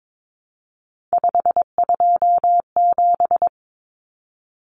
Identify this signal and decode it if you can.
Morse code, heard as a keyed tone, about 22 wpm, 700 Hz. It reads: 527
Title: Numbers - 3-Digit 22wpm